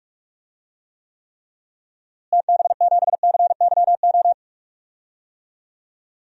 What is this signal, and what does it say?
Morse code, heard as a keyed tone, about 44 wpm, 700 Hz. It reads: TB7CYO
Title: DXCC-100 Call Signs - 6 Characters - Repeated 3x - 45wpm